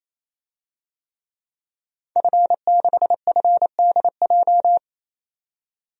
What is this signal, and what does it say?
Morse code, heard as a keyed tone, about 28 wpm, 700 Hz. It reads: F6FDJ
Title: DXCC-100 Call Signs - 5 Characters - 28wpm